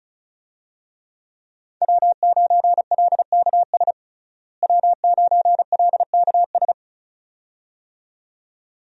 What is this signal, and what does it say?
Morse code, heard as a keyed tone, about 34 wpm, 700 Hz. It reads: W9LKS W9LKS
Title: USA Call Signs - 5 Characters - Repeated 2x - 35wpm